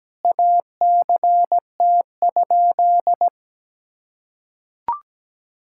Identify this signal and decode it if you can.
Morse code, heard as a keyed tone, about 17 wpm, 700 Hz. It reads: ACT?
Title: Sentences from Top 1000 Words - Review Part 6 - 17wpm